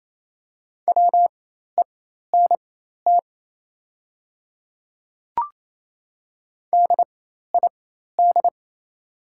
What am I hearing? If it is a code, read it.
Morse code, heard as a keyed tone, about 28 wpm, spaced as for 13 wpm, 700 Hz. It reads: WENT DID
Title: 4x Character Spacing - Top 300 Words - 28wpm